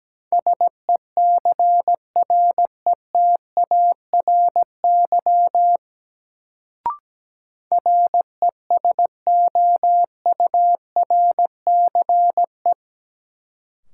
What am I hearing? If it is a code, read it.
Morse code, heard as a keyed tone, about 17 wpm, 700 Hz. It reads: SECRETARY RESOURCE
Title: Sentences from Top 1300 Words - New Word Drill - 17wpm